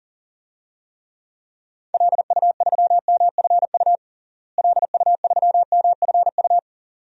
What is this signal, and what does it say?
Morse code, heard as a keyed tone, about 40 wpm, 700 Hz. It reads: LU3MFU LU3MFU
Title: DXCC-100 Call Signs - 6 Characters - Repeated 2x - 40wpm